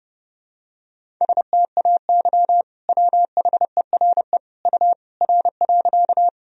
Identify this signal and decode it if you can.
Morse code, heard as a keyed tone, about 32 wpm, 700 Hz. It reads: STAY WHERE U R.
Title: Sentences from Top 500 Words and Abbreviations 30wpm